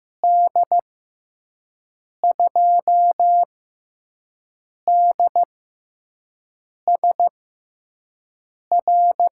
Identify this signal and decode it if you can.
Morse code, heard as a keyed tone, about 15 wpm, spaced as for 5 wpm, 700 Hz. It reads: D2DSR
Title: USA Call Signs - 6 Characters - 6x Character Spacing - 15wpm